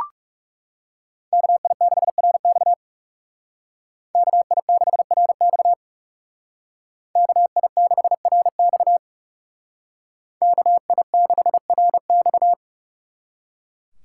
Morse code, heard as a keyed tone, about 36 wpm, 700 Hz. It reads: KI6RX KI6RX KI6RX KI6RX
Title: Speed Racing - USA 2x2 Call Signs 30wpm